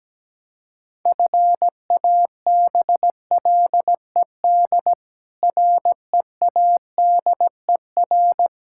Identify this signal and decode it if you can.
Morse code, heard as a keyed tone, about 17 wpm, 700 Hz. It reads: FABLED READER
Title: Pangram Challenge v2 - 17wpm